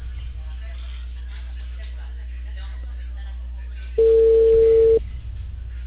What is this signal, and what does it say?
Ringback tone (China, 450 Hz), with speech.